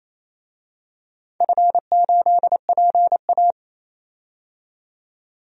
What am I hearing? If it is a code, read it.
Morse code, heard as a keyed tone, about 27 wpm, 700 Hz. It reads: F8PA
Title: DXCC-100 Call Signs - 4 Characters - 28wpm